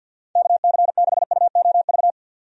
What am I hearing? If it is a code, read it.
Morse code, heard as a keyed tone, about 52 wpm, 700 Hz. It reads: KC6UOV
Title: USA Call Signs - 6 Characters - 50wpm